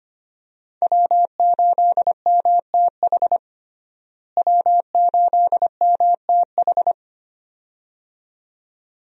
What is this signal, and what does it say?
Morse code, heard as a keyed tone, about 25 wpm, 700 Hz. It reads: W8MTH W8MTH
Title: USA Call Signs - 5 Characters - Repeated 2x - 25wpm